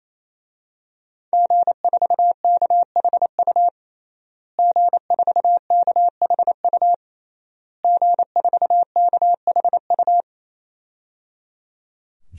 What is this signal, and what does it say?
Morse code, heard as a keyed tone, about 27 wpm, 700 Hz. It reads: G4KHU G4KHU G4KHU
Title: DXCC-100 Call Signs - 5 Characters - Repeated 3x - 28wpm